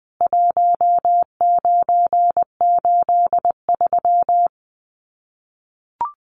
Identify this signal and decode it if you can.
Morse code, heard as a keyed tone, about 20 wpm, 700 Hz. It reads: 1983
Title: Common QSO Phrases 20wpm